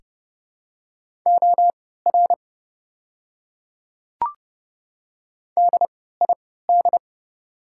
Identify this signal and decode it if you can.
Morse code, heard as a keyed tone, about 32 wpm, spaced as for 17 wpm, 700 Hz. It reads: OR DID
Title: 3x Character Spacing - Top 100 Words - 30wpm